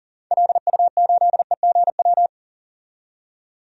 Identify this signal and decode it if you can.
Morse code, heard as a keyed tone, about 40 wpm, 700 Hz. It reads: LU8EGW
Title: DXCC-100 Call Signs - 6 Characters - 40wpm